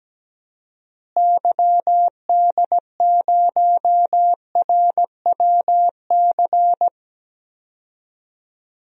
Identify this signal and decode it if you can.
Morse code, heard as a keyed tone, about 17 wpm, 700 Hz. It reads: YD0RWC
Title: DXCC-100 Call Signs - 6 Characters - Repeated 2x - 17wpm